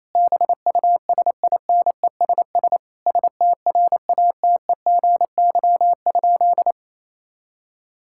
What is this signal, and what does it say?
Morse code, heard as a keyed tone, about 28 wpm, 700 Hz. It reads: BUSINESS STRATEGY?
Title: Sentences from Top 1000 Words - Review Part 6 - 28wpm